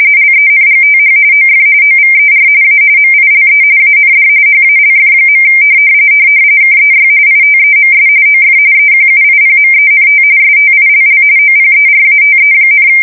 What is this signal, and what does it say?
RTTY, about 45.45 baud. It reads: LENGTH OF OVER A MILE AND HALF. KEY ITEMS OF EQUIPMENT STILL OPERATE,
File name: fsk170 / battleshipnj